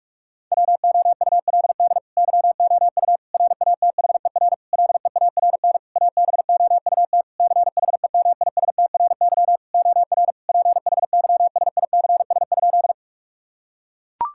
Morse code, heard as a keyed tone, about 42 wpm, 700 Hz. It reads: WOULD YOU RATHER LEARN ABOUT CHEMISTRY OR PHYSICS?
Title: Sentences from Top 1000 Words - Review Part 6 - 45wpm